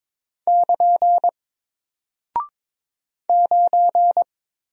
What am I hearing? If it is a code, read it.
Morse code, heard as a keyed tone, about 22 wpm, 700 Hz. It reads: <KN> 9
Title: Single Character (Letter:Number:Punctuation:Prosign) Alt 22wpm